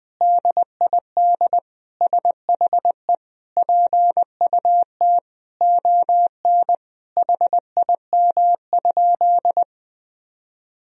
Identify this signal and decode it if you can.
Morse code, heard as a keyed tone, about 20 wpm, 700 Hz. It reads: DID SHE PUT ON HIM?
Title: Sentences from Top 1000 Words - Review Part 1 - 20wpm